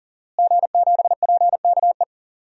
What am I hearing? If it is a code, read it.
Morse code, heard as a keyed tone, about 40 wpm, 700 Hz. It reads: G7PKE
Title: DXCC-100 Call Signs - 5 Characters - Repeated 3x - 40wpm